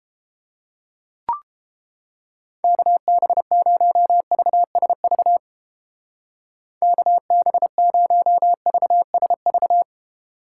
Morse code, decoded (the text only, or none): KB0VSV KB0VSV